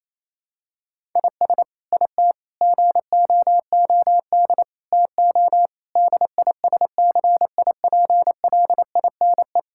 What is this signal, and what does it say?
Morse code, heard as a keyed tone, about 28 wpm, 700 Hz. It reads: IS IT GOOD TO DISCIPLINE
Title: Sentences from Top 1000 Words - Review Part 6 - 28wpm